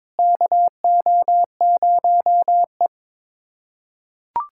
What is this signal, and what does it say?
Morse code, heard as a keyed tone, about 21 wpm, 700 Hz. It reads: KO0E
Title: USA Callsign DE Callsign - 4 Characters - 22wpm